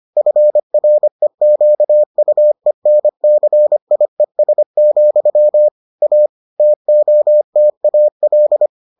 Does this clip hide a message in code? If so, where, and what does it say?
Yes: Morse code from 0.2 to 8.7 s, heard as a keyed tone, about 25 wpm, 600 Hz. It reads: FREQUENCIES, A TOTAL